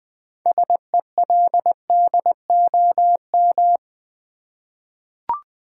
Morse code, heard as a keyed tone, about 20 wpm, 700 Hz. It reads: SELDOM
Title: 6 Letter Words - First 3 Letters Given - 20wpm